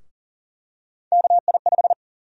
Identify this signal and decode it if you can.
Morse code, heard as a keyed tone, about 40 wpm, 700 Hz. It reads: KI5
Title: USA Call Sign Prefixes - 40wpm